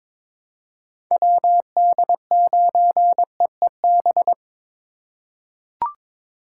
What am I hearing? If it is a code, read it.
Morse code, heard as a keyed tone, about 22 wpm, 700 Hz. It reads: WD9EEB
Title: USA Call Signs - 6 Characters - 3x Character Spacing - 22wpm